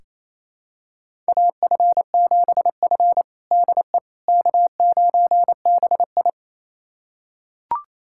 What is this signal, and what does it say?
Morse code, heard as a keyed tone, about 28 wpm, 700 Hz. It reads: AF7F DE K9BI
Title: USA Callsign DE Callsign - 4 Characters - 28wpm